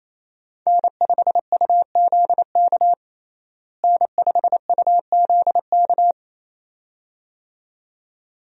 Morse code, heard as a keyed tone, about 28 wpm, 700 Hz. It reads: N5UZK N5UZK
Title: USA Call Signs - 5 Characters - Repeated 2x - 28wpm